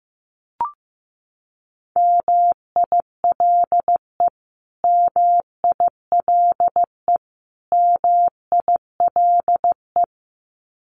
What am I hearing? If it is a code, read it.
Morse code, heard as a keyed tone, about 15 wpm, 700 Hz. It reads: MILE MILE MILE
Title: Top 500 Words Repeated 15wpm